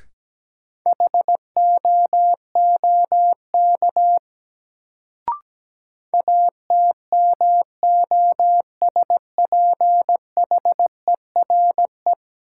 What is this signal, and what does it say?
Morse code, heard as a keyed tone, about 17 wpm, 700 Hz. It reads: HOOK ATMOSPHERE
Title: Sentences from Top 1600 Words - New Word Drill Alt - 17wpm